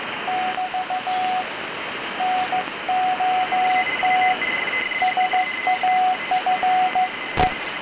Morse code, heard as a keyed tone, about 15 wpm, 740 Hz, with radio noise. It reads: <BT> NOT SAFE